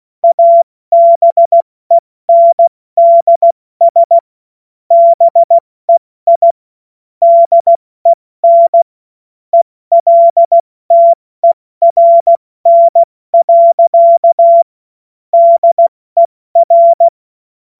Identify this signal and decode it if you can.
Morse code, heard as a keyed tone, about 16 wpm, 680 Hz. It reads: ABENDS BEI DEN ELTERN. DER
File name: Morsekurs / Witze / Witz 05 März 2024 70 F80